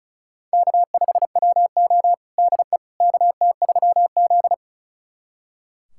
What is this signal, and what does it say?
Morse code, heard as a keyed tone, about 36 wpm, 700 Hz. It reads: K5WO DE KT3Z
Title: USA Callsign DE Callsign - 4 Characters - 35wpm